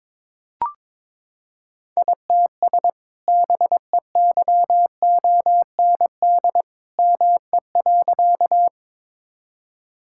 Morse code, heard as a keyed tone, about 22 wpm, 700 Hz. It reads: ITS BEYOND ME.
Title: Sentences from Top 1000 Words - Review Part 6 - 22wpm